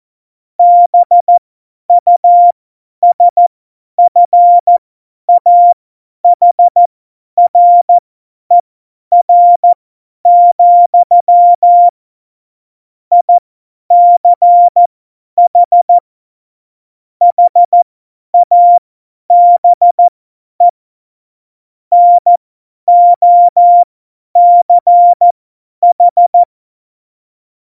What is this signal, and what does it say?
Morse code, heard as a keyed tone, about 14 wpm, spaced as for 10 wpm, 700 Hz. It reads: BUSFAHRER, ICH HABE NOCH
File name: Morsekurs / Witze / Witz 27. November 2018, 50 F70